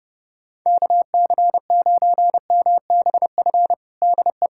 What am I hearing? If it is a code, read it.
Morse code, heard as a keyed tone, about 30 wpm, 700 Hz. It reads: KC9MBF DE
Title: USA Callsign DE Callsign - 6 Characters - 30wpm